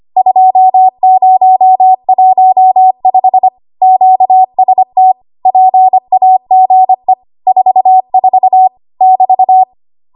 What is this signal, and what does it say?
Morse code, heard as a keyed tone, about 25 wpm, 750 Hz. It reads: 2015 QST PAGE 44 <BT>
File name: Archive / 25 WPM / 25WPM